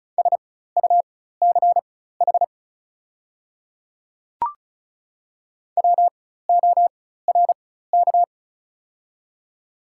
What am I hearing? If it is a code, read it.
Morse code, heard as a keyed tone, about 36 wpm, spaced as for 16 wpm, 700 Hz. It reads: SUCH WORK
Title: 4x Character Spacing - Top 200 Words - 35wpm